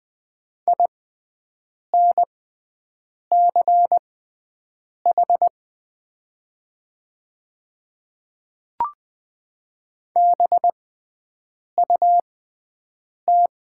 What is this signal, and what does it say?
Morse code, heard as a keyed tone, about 20 wpm, spaced as for 7 wpm, 700 Hz. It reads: INCH BUT
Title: 6x Character Spacing - Top 500 Words - 20wpm